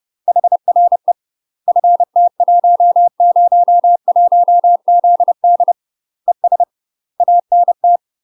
Morse code, heard as a keyed tone, about 30 wpm, 700 Hz. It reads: HRE FT101ZD ES ANT
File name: UK MASTER QSOs TRAINING / 30WPM / UK MASTER QSOs 0021